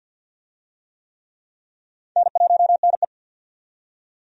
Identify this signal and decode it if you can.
Morse code, heard as a keyed tone, about 48 wpm, 700 Hz. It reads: N1NE
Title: USA Call Signs - 4 Characters - 50wpm